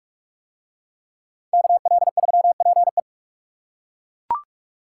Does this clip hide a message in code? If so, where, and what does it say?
Yes: Morse code from 1.5 to 3.0 s, heard as a keyed tone, about 41 wpm, 700 Hz. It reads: KL3PE